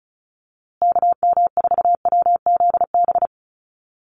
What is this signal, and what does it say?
Morse code, heard as a keyed tone, about 34 wpm, 700 Hz. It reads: KM4WZB
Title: USA Call Signs Repeated 3x 35wpm